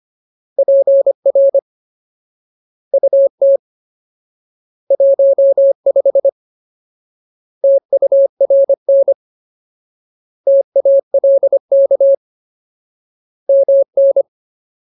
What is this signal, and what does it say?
Morse code, heard as a keyed tone, about 25 wpm, 550 Hz. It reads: PR UT 15 TURN TALK MN
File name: Ultimate Word List / 25WPM Ultimate